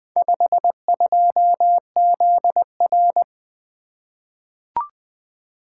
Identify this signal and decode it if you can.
Morse code, heard as a keyed tone, about 20 wpm, 700 Hz. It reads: 52ZR
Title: Four Letter-Number Combinations - 20wpm